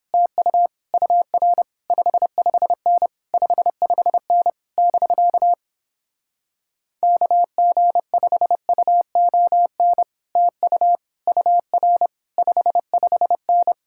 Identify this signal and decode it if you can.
Morse code, heard as a keyed tone, about 29 wpm, 700 Hz. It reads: TU UR 55N 55N <BK> KG5UON TU UR 55N